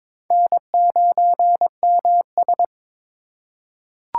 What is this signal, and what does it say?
Morse code, heard as a keyed tone, about 22 wpm, 700 Hz. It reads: N9MS
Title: K1USN SST QSO Practice - 22wpm